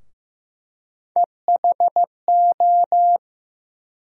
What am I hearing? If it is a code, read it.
Morse code, heard as a keyed tone, about 16 wpm, 700 Hz. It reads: EHO
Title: USA Call Sign Postfixes - 3 Letters 15wpm